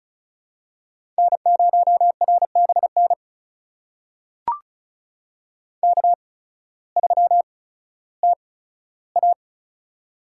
Morse code, heard as a keyed tone, about 36 wpm, spaced as for 16 wpm, 700 Hz. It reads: N0RBN K3TA
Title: USA Call Signs - 8x Character Spacing - 35wpm